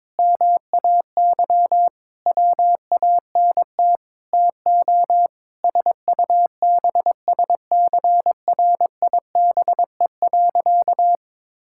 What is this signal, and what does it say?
Morse code, heard as a keyed tone, about 22 wpm, 700 Hz. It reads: MAY WANT TO SUBSCRIBE.
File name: practice / 22wpm / Sentences with 8-Word Phrases - 22wpm